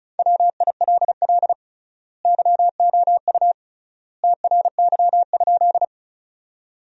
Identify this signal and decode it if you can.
Morse code, heard as a keyed tone, about 34 wpm, 700 Hz. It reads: WILL YOU TRY?
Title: 3x Word-Spacing - Sentences from Top 200 Words - 35wpm